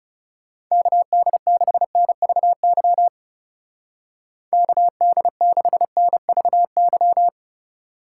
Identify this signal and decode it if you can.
Morse code, heard as a keyed tone, about 40 wpm, 700 Hz. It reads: KD6NVY KD6NVY